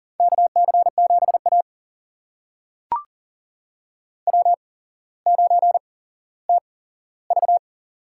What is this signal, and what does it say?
Morse code, heard as a keyed tone, about 40 wpm, spaced as for 17 wpm, 700 Hz. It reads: KC7A W9TV
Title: USA Call Signs - 4 Characters - 8x Character Spacing - 40wpm